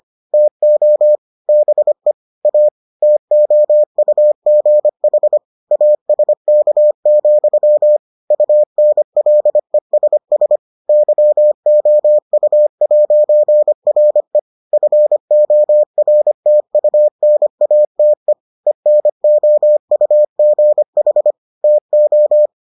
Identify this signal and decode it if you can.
Morse code, heard as a keyed tone, about 25 wpm, 600 Hz. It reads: TO BE A TOUGH ASK, UNLESS YOU'RE FORTUNATE ENOUGH TO